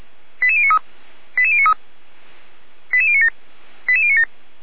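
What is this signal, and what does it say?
Five-tone selective call (ZVEI), heard as quick runs of five tones, tones of about 70 ms. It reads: 88093 88093 88097 88097
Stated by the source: ZVEI five-tone call, address 88097, received on an SDR.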